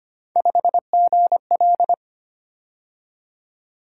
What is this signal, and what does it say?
Morse code, heard as a keyed tone, about 25 wpm, 700 Hz. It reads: 5GL
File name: Three Letter-Number Combinations - 25wpm